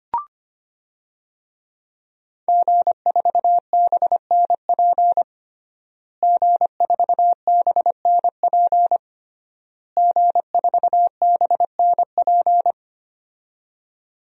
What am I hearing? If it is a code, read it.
Morse code, heard as a keyed tone, about 25 wpm, 700 Hz. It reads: G4BNP G4BNP G4BNP
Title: DXCC-100 Call Signs - 5 Characters - Repeated 3x - 25wpm